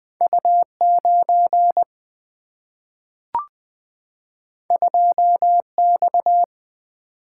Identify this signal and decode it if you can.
Morse code, heard as a keyed tone, about 20 wpm, 700 Hz. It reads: U9 2X
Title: Single-Letter Single-Number 20wpm